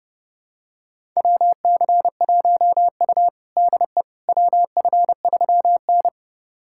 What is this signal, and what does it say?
Morse code, heard as a keyed tone, about 30 wpm, 700 Hz. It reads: WC1U DE WF3N
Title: USA Callsign DE Callsign - 4 Characters - 30wpm